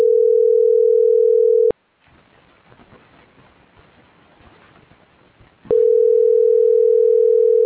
Ringback tone (North America), 440 + 480 Hz.